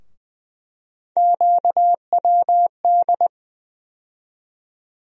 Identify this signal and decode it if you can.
Morse code, heard as a keyed tone, about 20 wpm, 700 Hz. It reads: QWD
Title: USA Call Sign Postfixes - 3 Letters 20wpm